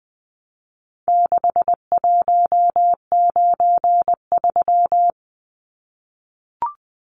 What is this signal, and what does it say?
Morse code, heard as a keyed tone, about 20 wpm, 700 Hz. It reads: 6193
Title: Numbers - 4-Digit 20wpm